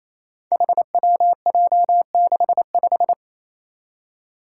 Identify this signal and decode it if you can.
Morse code, heard as a keyed tone, about 28 wpm, 700 Hz. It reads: HWJ65
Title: Five Challenging Letter-Number Combinations 28wpm